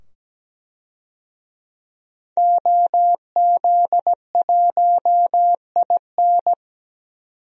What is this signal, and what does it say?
Morse code, heard as a keyed tone, about 17 wpm, 700 Hz. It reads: OZ1IN
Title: DXCC-100 Call Signs - 5 Characters - 17wpm